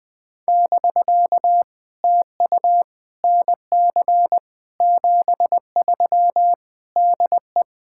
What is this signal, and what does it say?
Morse code, heard as a keyed tone, about 20 wpm, 700 Hz. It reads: <BK> TU NC 73 DE
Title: POTA QSO Practice - 20wpm